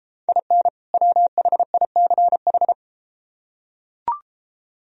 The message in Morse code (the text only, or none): IN WHICH